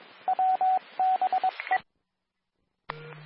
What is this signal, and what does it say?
Morse code, heard as a keyed tone, about 22 wpm, 750 Hz, with radio noise. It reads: WBE
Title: Nassau and Suffolk County Amateur Repeaters